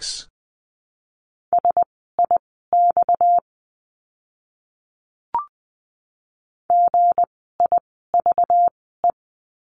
Morse code, heard as a keyed tone, about 20 wpm, spaced as for 15 wpm, 700 Hz. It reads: SIX GIVE
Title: 230 Lesson 37f - Letter x - Words 20wpm